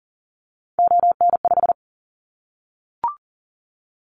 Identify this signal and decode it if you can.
Morse code, heard as a keyed tone, about 40 wpm, 700 Hz. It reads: ON5
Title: Callsign Prefixes for Top 25 DXCC - Repeated 3x - 40wpm